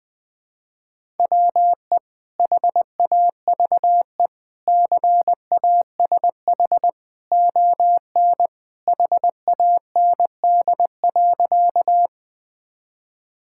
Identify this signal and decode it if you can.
Morse code, heard as a keyed tone, about 19 wpm, 700 Hz. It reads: WE HAVE CASH ON HAND.
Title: Sentences from Top 1000 Words - Review Part 5 - 20wpm